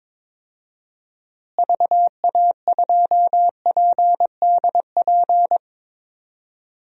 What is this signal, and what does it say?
Morse code, heard as a keyed tone, about 22 wpm, 700 Hz. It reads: VA2PDP
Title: DXCC-100 Call Signs - 6 Characters - Repeated 2x - 22wpm